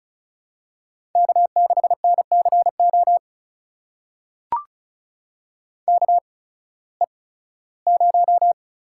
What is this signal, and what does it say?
Morse code, heard as a keyed tone, about 34 wpm, spaced as for 18 wpm, 700 Hz. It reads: K6NCO KE0
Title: USA Call Signs - 8x Character Spacing - 35wpm